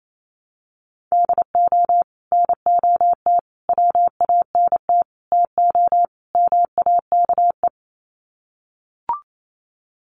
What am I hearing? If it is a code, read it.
Morse code, heard as a keyed tone, about 28 wpm, 690 Hz. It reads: DO NOT WANT TO MAKE
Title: Sets of 5 Words Encore 28wpm